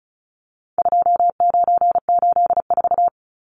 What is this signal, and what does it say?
Morse code, heard as a keyed tone, about 34 wpm, 700 Hz. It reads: 2984
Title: Numbers - 4-Digit 35wpm